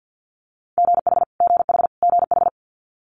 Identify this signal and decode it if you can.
Morse code, heard as a keyed tone, about 50 wpm, 700 Hz. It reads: GH GH GH
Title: Multi-letter Phonemes Repeated 3x 50wpm